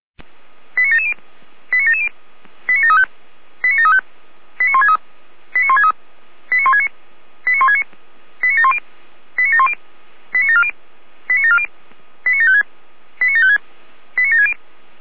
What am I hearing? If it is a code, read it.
Five-tone selective call (ZVEI), heard as quick runs of five tones, tones of about 80 ms. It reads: 89770 89770 89735 89735 89163 89163 89169 89169 89710 89710 89740 89740 89756 89756 89760